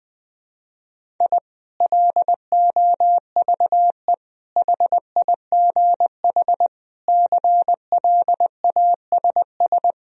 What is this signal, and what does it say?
Morse code, heard as a keyed tone, about 20 wpm, 700 Hz. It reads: I LOVE HIGH CLASS
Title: Sentences from Top 1000 Words - Review Part 4 - 20wpm